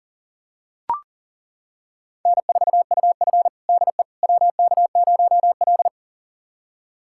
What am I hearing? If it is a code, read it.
Morse code, heard as a keyed tone, about 37 wpm, 700 Hz. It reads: N4UF DE WK0L
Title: USA Callsign DE Callsign - 4 Characters - 40wpm